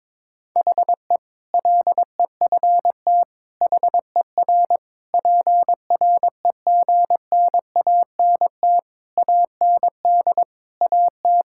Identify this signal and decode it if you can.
Morse code, heard as a keyed tone, about 22 wpm, 700 Hz. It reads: HE LEFT HER PREGNANT AND AT
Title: Sentences from Top 1000 Words - Review Part 6 - 22wpm